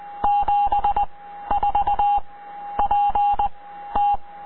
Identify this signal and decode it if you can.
Morse code, heard as a keyed tone, about 20 wpm, 820 Hz. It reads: 7 4 P T